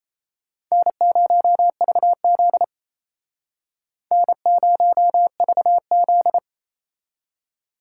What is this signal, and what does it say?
Morse code, heard as a keyed tone, about 35 wpm, 700 Hz. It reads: N0VZ N0VZ